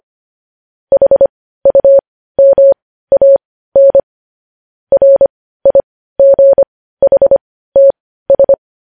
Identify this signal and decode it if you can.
Morse code, heard as a keyed tone, about 25 wpm, spaced as for 15 wpm, 550 Hz. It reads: HUMAN RIGHTS
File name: podcast / News headlines-1x-25x15